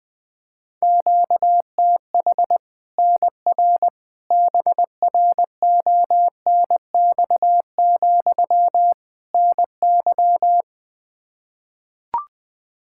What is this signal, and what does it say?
Morse code, heard as a keyed tone, about 20 wpm, 700 Hz. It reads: QTH NR BRONX, NY
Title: Practice QSO - Exchange 1 - Beginning - 20wpm